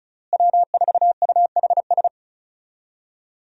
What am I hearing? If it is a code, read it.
Morse code, heard as a keyed tone, about 34 wpm, 700 Hz. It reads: W4UHS
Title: USA Call Signs - 5 Characters - Repeated 2x - 35wpm